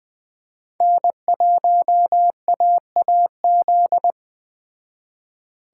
Morse code, heard as a keyed tone, about 21 wpm, 700 Hz. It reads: N1AAZ